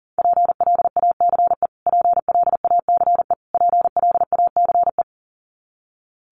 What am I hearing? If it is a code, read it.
Morse code, heard as a keyed tone, about 40 wpm, 700 Hz. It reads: PLACE PLACE PLACE
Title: Top 500 Words Repeated 40wpm